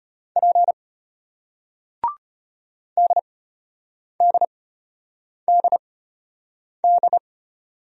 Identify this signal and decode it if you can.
Morse code, heard as a keyed tone, about 33 wpm, spaced as for 8 wpm, 700 Hz. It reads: P DDDD